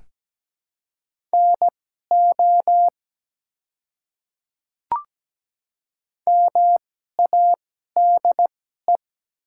Morse code, heard as a keyed tone, about 17 wpm, spaced as for 12 wpm, 700 Hz. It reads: NO MADE
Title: 2x Character Spacing - Top 300 Words - 17wpm